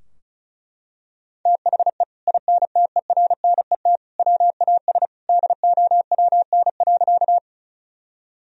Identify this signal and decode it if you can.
Morse code, heard as a keyed tone, about 36 wpm, 700 Hz. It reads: THE INTERNET WAS DOWN.
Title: Sentences from Top 1000 Words - Review Part 4 - 35wpm